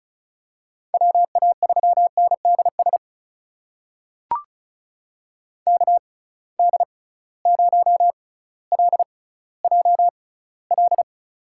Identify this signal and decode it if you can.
Morse code, heard as a keyed tone, about 34 wpm, spaced as for 18 wpm, 700 Hz. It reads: WA3NDS KD0LJL
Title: USA Call Signs - 6 Characters - 6x Character Spacing - 35wpm